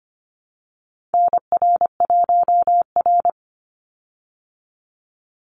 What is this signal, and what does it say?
Morse code, heard as a keyed tone, about 24 wpm, 700 Hz. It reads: NR1R
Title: Call Signs Repeated 3x - 25wpm